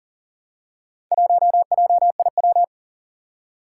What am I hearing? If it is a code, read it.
Morse code, heard as a keyed tone, about 38 wpm, 700 Hz. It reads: 1JIW
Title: Four Challenging Letter-Number Combinations 40wpm